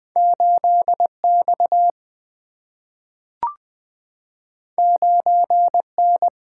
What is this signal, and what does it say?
Morse code, heard as a keyed tone, about 20 wpm, 700 Hz. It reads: 8 X 9 N
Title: Single-Letter Single-Number 20wpm